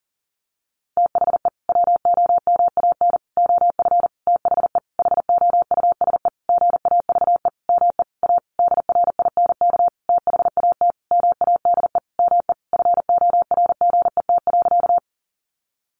Morse code, heard as a keyed tone, about 37 wpm, 700 Hz. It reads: THE WOMAN OF THE HOUSE GAVE ME A DRINK THAT MADE ME FORGET.
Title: Sentences from Top 1500 Words 40wpm